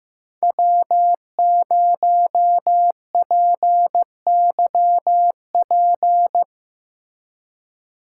Morse code, heard as a keyed tone, about 17 wpm, 700 Hz. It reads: W0PYP
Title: Speed Racing - USA 1x3 Call Signs 15wpm